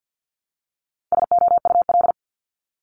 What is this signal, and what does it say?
Morse code, heard as a keyed tone, about 50 wpm, 700 Hz. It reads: SOUL
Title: Multi-letter Phonemes 50wpm